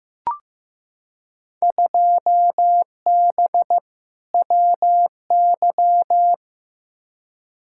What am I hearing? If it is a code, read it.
Morse code, heard as a keyed tone, about 15 wpm, 700 Hz. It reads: 2B WY
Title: Field Day Exchanges Encore - 15wpm